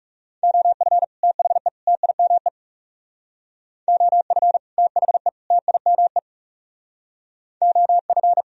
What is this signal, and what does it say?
Morse code, heard as a keyed tone, about 37 wpm, 700 Hz. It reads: OF THE TIME OF THE TIME OF
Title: Speed-Racing - Sets of 3 Words 30wpm